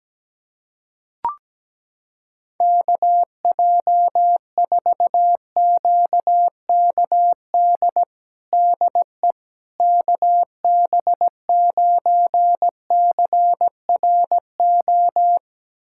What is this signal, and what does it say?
Morse code, heard as a keyed tone, about 17 wpm, 700 Hz. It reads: KJ4QKD DE KB9CRO
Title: USA Callsign DE Callsign - 6 Characters - 17wpm